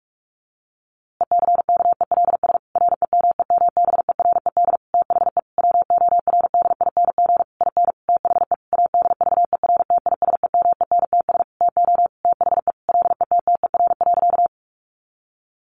Morse code, heard as a keyed tone, about 44 wpm, 700 Hz. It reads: ECKELS REMEMBERED THE WORDING IN THE ADVERTISEMENTS TO THE LETTER.
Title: A Sound Of Thunder 45wpm